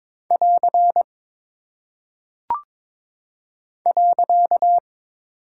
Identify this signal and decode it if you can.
Morse code, heard as a keyed tone, about 22 wpm, 700 Hz. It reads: <AR> .
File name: practice / 22wpm / Single Punctuation-Prosign - 22wpm